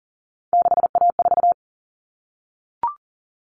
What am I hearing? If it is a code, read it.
Morse code, heard as a keyed tone, about 40 wpm, 700 Hz. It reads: 6A4
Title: Common Cut Numbers - 3-Digit 40wpm